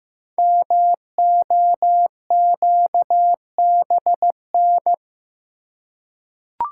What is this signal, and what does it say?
Morse code, heard as a keyed tone, about 15 wpm, 700 Hz. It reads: MOQBN
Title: Code Groups - 5 Letters - 15wpm